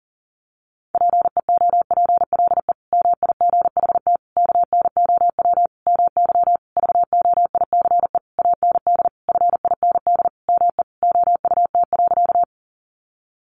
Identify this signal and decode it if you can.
Morse code, heard as a keyed tone, about 40 wpm, 700 Hz. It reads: PEOPLE MIGHT KNOW MY VOICE AND FIND ME OUT.
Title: Sentences from Top 500 Words 40wpm